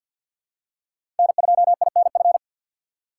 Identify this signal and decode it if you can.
Morse code, heard as a keyed tone, about 53 wpm, 690 Hz. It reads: N2INF
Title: USA Call Signs - 6x Character Spacing - 50wpm